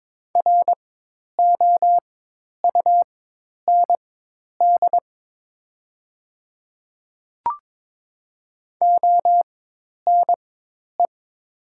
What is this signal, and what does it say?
Morse code, heard as a keyed tone, about 22 wpm, spaced as for 10 wpm, 700 Hz. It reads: ROUND ONE
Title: 4x Character Spacing - Top 200 Words - 22wpm